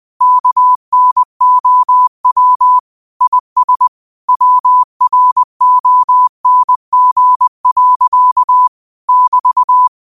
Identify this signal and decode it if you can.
Morse code, heard as a keyed tone, about 20 wpm, 1000 Hz. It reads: KNOW IS WRONG. <BT>
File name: qotdfiles / 20wpm